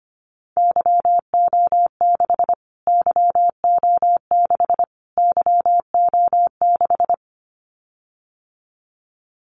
Morse code, heard as a keyed tone, about 25 wpm, 700 Hz. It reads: YO6 YO6 YO6
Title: Callsign Prefixes for Top 25 DXCC - Repeated 3x - 25wpm